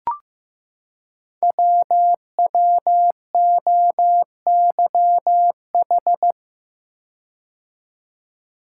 Morse code, heard as a keyed tone, about 15 wpm, 700 Hz. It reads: WWOYH
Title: Code Groups - 5 Letters - 15wpm